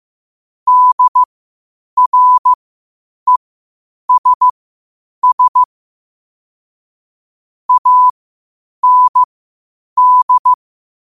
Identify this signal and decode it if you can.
Morse code, heard as a keyed tone, about 15 wpm, spaced as for 8 wpm, 1000 Hz. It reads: DRESS AND
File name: 8wpm